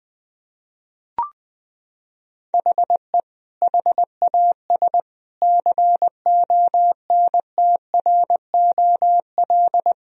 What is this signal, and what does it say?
Morse code, heard as a keyed tone, about 20 wpm, 700 Hz. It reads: HE HAS CONTROL
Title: Sentences from Top 1000 Words - Review Part 3 - 20wpm